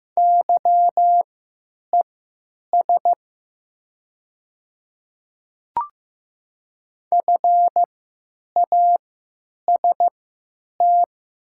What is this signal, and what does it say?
Morse code, heard as a keyed tone, about 15 wpm, spaced as for 9 wpm, 700 Hz. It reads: YES FAST
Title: 3x Character Spacing - Top 500 Words - 15wpm